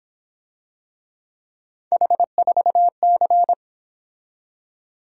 Morse code, heard as a keyed tone, about 26 wpm, 700 Hz. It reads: H4C
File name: Speed-Racing - Three Letter-Number Combinations - 22wpm